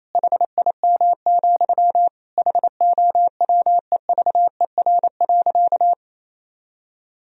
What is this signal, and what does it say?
Morse code, heard as a keyed tone, about 28 wpm, 700 Hz. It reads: HIM, HOWEVER.
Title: Sentences from Top 1000 Words - Review Part 2 - 28wpm